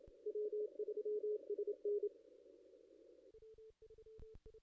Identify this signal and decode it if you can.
Morse code, heard as a keyed tone, about 30 wpm, 410 Hz. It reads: W3SN
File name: contests / ss-cw-k5tr-op-n5rz -1600